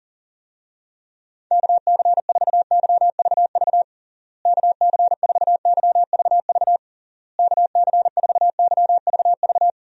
Morse code, heard as a keyed tone, about 40 wpm, 700 Hz. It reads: KC4YVV KC4YVV KC4YVV
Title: USA Call Signs - 6 Characters - Repeated 3x - 40wpm